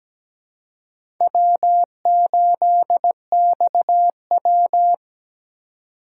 Morse code, heard as a keyed tone, about 20 wpm, 700 Hz. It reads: W8XW
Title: Speed Racing - USA 1x2 Call Signs 17wpm